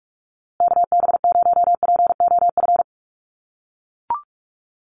Morse code, heard as a keyed tone, about 44 wpm, 700 Hz. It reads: KB0POF
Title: USA Call Signs Repeated 2x 45wpm